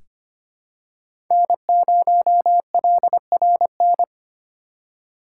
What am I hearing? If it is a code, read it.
Morse code, heard as a keyed tone, about 25 wpm, 700 Hz. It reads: N0LRN
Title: USA Call Signs - 5 Characters - 2x Character Spacing - 25wpm